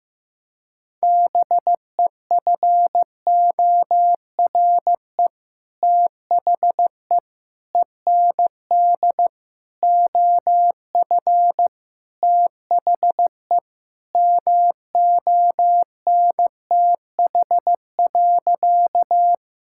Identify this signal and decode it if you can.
Morse code, heard as a keyed tone, about 15 wpm, 700 Hz. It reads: BEFORE THE END OF THE MONTH.
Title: Sentences from Top 1000 Words - Review Part 5 - 15wpm